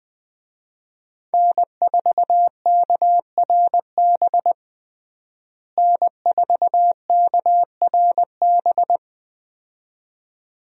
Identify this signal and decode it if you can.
Morse code, heard as a keyed tone, about 20 wpm, 700 Hz. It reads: N4KRB N4KRB
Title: USA Call Signs - 5 Characters - Repeated 2x - 20wpm